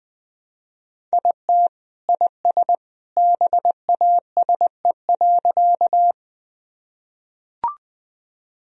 Morse code, heard as a keyed tone, about 20 wpm, 700 Hz. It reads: IT IS BASE.
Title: Sentences from Top 1000 Words - Review Part 1 - 20wpm